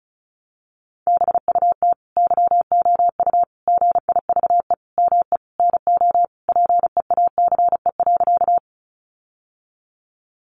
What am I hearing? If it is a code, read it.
Morse code, heard as a keyed tone, about 34 wpm, 700 Hz. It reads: BUT YOU GIVE ME NO PEACE.
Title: Sentences from Top 1500 Words 35wpm